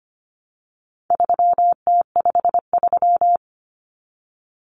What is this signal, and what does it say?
Morse code, heard as a keyed tone, about 25 wpm, 700 Hz. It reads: 3T53
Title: Common Cut Numbers - 4-Digit 25wpm